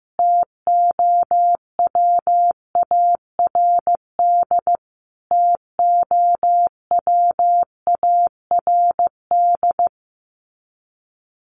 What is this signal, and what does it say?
Morse code, heard as a keyed tone, about 15 wpm, 700 Hz. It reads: TOWARD TOWARD
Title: Top 500 Words Repeated 15wpm